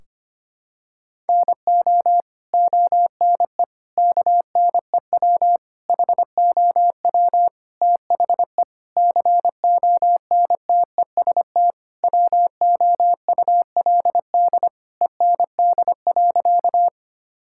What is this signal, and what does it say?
Morse code, heard as a keyed tone, about 25 wpm, 700 Hz. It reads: NO ONE KNEW HOW THE CONTEST WOULD END.
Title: Sentences from Top 1600 Words 25wpm